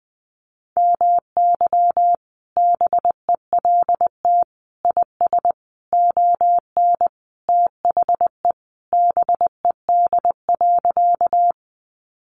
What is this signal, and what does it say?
Morse code, heard as a keyed tone, about 20 wpm, 700 Hz. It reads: MY BELT IS ON THE BED.
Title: Sentences from Top 1400 Words 20wpm